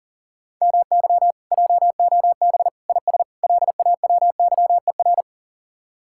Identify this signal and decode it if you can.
Morse code, heard as a keyed tone, about 52 wpm, 700 Hz. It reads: MY JOB IS LAWYER